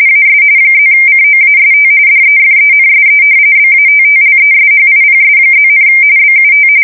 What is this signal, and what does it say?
RTTY, about 45.45 baud. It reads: AS +COAST DEFENSE BATTLESHIPS+, BUT
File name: fsk170 / battleships